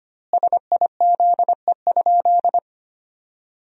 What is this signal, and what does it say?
Morse code, heard as a keyed tone, about 25 wpm, 700 Hz. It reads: SIZE?
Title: 2x Word-Spacing - Sentences from Top 500 Words - 25wpm